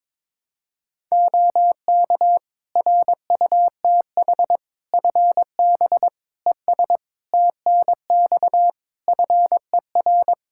Morse code, heard as a keyed tone, about 22 wpm, 700 Hz. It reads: OK RUTH FB ES TNX FER
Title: Practice QSO - Exchange 2 - Info - 22wpm